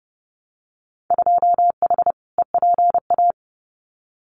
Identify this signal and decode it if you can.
Morse code, heard as a keyed tone, about 30 wpm, 700 Hz. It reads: 2H EPA
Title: Winter Field Day Exchanges Encore 30wpm